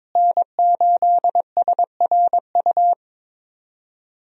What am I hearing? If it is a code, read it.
Morse code, heard as a keyed tone, about 22 wpm, 700 Hz. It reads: N8SRU
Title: USA Call Signs - 5 Characters - Repeated 2x - 22wpm